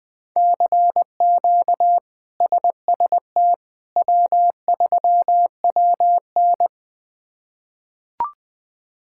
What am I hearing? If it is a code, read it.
Morse code, heard as a keyed tone, about 21 wpm, 700 Hz. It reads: CQ SST W3WN
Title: K1USN SST QSO Practice - 20wpm